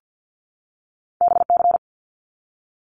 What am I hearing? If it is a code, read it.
Morse code, heard as a keyed tone, about 50 wpm, 700 Hz. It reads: BC
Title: Canadian Provinces and Territories 50wpm